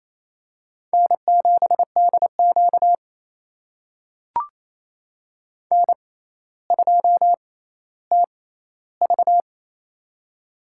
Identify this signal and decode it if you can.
Morse code, heard as a keyed tone, about 28 wpm, spaced as for 14 wpm, 700 Hz. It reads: N7DQ N2TV
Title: USA Call Signs - 4 Characters - 6x Character Spacing - 28wpm